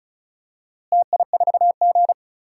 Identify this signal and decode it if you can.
Morse code, heard as a keyed tone, about 36 wpm, 700 Hz. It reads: TI4G
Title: DXCC-100 Call Signs - 4 Characters - Repeated 3x - 35wpm